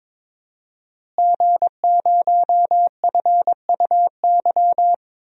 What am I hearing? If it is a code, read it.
Morse code, heard as a keyed tone, about 22 wpm, 700 Hz. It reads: G0FUY
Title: DXCC-100 Call Signs - 5 Characters - Repeated 2x - 22wpm